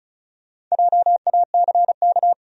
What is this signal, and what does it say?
Morse code, heard as a keyed tone, about 34 wpm, 700 Hz. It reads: JACK
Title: USA Common Names - Max 4 Letters - 35wpm